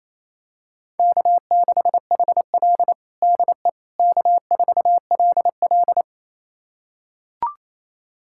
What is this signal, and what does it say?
Morse code, heard as a keyed tone, about 27 wpm, 700 Hz. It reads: K6HL DE K4LL
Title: USA Callsign DE Callsign - 4 Characters - 28wpm